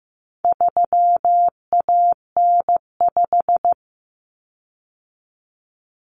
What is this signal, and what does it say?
Morse code, heard as a keyed tone, about 15 wpm, 700 Hz. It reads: 3AN5
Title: Common Cut Numbers - 4-Digit 15wpm